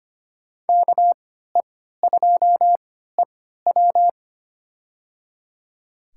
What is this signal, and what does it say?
Morse code, heard as a keyed tone, about 25 wpm, spaced as for 14 wpm, 700 Hz. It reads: KE2EW
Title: USA Call Signs - 5 Characters - 3x Character Spacing - 25wpm